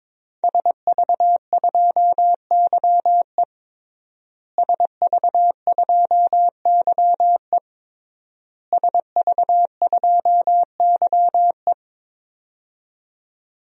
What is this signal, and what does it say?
Morse code, heard as a keyed tone, about 21 wpm, 700 Hz. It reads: SV2YE SV2YE SV2YE
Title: DXCC-100 Call Signs - 5 Characters - Repeated 3x - 22wpm